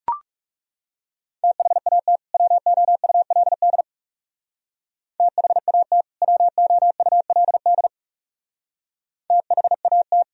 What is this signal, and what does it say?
Morse code, heard as a keyed tone, about 37 wpm, 700 Hz. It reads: THAT WOULD THAT WOULD THAT
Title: Speed-Racing - Sets of 2 Words 30wpm